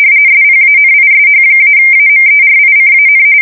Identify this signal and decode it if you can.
RTTY, about 45.45 baud. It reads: SPANISH-AMERICAN
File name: fsk170 / usnavy3